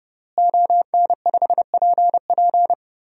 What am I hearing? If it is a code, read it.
Morse code, heard as a keyed tone, about 31 wpm, 700 Hz. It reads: ON5PP
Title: DXCC-100 Call Signs - 5 Characters - Repeated 2x - 30wpm